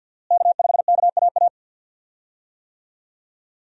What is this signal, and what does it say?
Morse code, heard as a keyed tone, about 47 wpm, 700 Hz. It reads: K5KAA
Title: USA Call Signs - 5 Characters - Repeated 2x - 50wpm